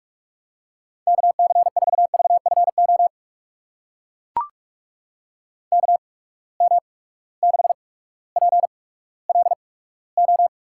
Morse code, heard as a keyed tone, about 44 wpm, spaced as for 19 wpm, 700 Hz. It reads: KC4VFO KM6PLO
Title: USA Call Signs - 8x Character Spacing - 45wpm